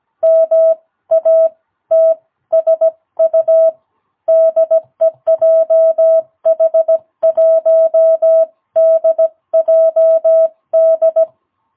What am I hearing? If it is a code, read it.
Morse code, heard as a keyed tone, about 17 wpm, spaced as for 14 wpm, 660 Hz. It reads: MATSU DE JH1DJD
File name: jh8cbh / 2022cw